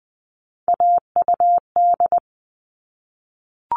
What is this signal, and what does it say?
Morse code, heard as a keyed tone, about 20 wpm, 700 Hz. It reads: AUD
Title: All Cut Numbers - 3-Digit 20wpm